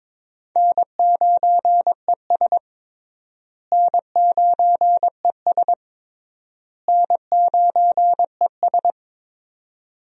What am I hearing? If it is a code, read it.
Morse code, heard as a keyed tone, about 22 wpm, 700 Hz. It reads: N9ES N9ES N9ES
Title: USA Call Signs - 4 Characters - Repeated 3x - 22wpm